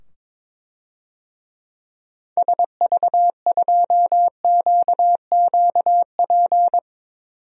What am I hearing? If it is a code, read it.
Morse code, heard as a keyed tone, about 22 wpm, 700 Hz. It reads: SV2QQP
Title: DXCC-100 Call Signs - 6 Characters - 22wpm